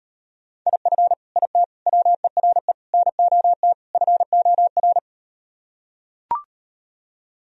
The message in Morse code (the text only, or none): IF IT WERE NOT FOR